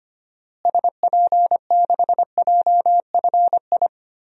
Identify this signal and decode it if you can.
Morse code, heard as a keyed tone, about 25 wpm, 700 Hz. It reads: SP6JFI
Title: DXCC-100 Call Signs - 6 Characters - Repeated 2x - 25wpm